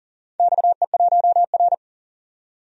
Morse code, heard as a keyed tone, about 39 wpm, 700 Hz. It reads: XE1R
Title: DXCC-100 Call Signs - 4 Characters - 40wpm